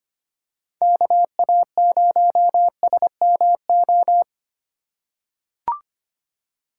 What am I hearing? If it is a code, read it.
Morse code, heard as a keyed tone, about 25 wpm, 700 Hz. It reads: KA0SMO
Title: USA Call Signs - 6 Characters - 4x Character Spacing - 25wpm